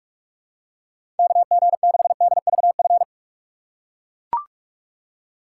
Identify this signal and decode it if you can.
Morse code, heard as a keyed tone, about 43 wpm, 700 Hz. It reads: KG6DVF
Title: USA Call Signs - 6 Characters - 4x Character Spacing - 45wpm